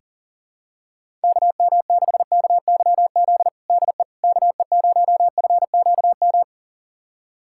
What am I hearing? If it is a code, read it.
Morse code, heard as a keyed tone, about 40 wpm, 700 Hz. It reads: KM6KYZ DE KE0FQM
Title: USA Callsign DE Callsign 40wpm